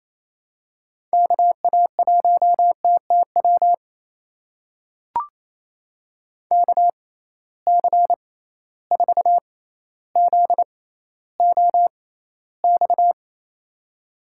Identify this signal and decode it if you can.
Morse code, heard as a keyed tone, about 28 wpm, spaced as for 14 wpm, 700 Hz. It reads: KA1TTW KC4ZOX
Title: USA Call Signs - 6 Characters - 6x Character Spacing - 28wpm